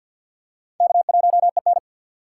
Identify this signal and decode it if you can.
Morse code, heard as a keyed tone, about 46 wpm, 700 Hz. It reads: K1EN
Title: USA Call Signs - 4 Characters - 3x Character Spacing - 50wpm